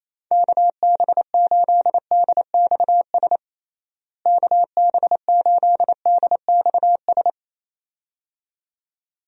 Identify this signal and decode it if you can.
Morse code, heard as a keyed tone, about 27 wpm, 700 Hz. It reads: KB8DXS KB8DXS
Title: USA Call Signs - 6 Characters - Repeated 2x - 28wpm